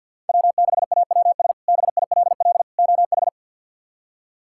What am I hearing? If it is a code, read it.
Morse code, heard as a keyed tone, about 53 wpm, 690 Hz. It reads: W6AWS BILL OH
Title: K1USN SST QSO Practice - 50wpm